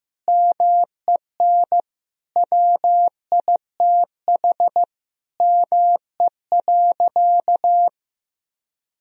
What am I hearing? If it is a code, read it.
Morse code, heard as a keyed tone, about 16 wpm, 700 Hz. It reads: MEN WITH ME.
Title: Sentences from Top 1000 Words - Review Part 2 - 15wpm